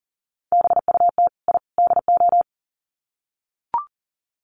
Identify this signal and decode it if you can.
Morse code, heard as a keyed tone, about 40 wpm, 700 Hz. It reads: BUT I DO
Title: Sets of 3 Words 40wpm